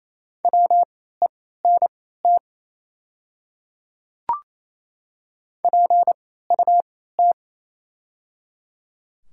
Morse code, heard as a keyed tone, about 27 wpm, spaced as for 16 wpm, 700 Hz. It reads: WENT PUT
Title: 3x Character Spacing - Top 200 Words - 28wpm